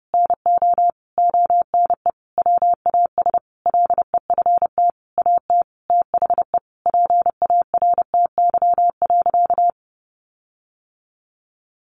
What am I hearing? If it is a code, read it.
Morse code, heard as a keyed tone, about 30 wpm, 700 Hz. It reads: NO ONE WAS LEFT AT THE PARTY.
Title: Sentences from Top 1400 Words 30wpm